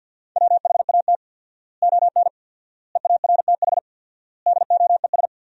Morse code, heard as a keyed tone, about 50 wpm, 700 Hz. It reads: WHAT ON EARTH DOES
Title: 4x Word-Spacing - Sentences from Top 200 Words - 50wpm